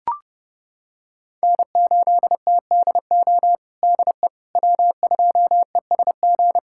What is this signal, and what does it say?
Morse code, heard as a keyed tone, about 30 wpm, 700 Hz. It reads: N8TDO DE W2ESG
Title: USA Callsign DE Callsign - 5 Characters - 30wpm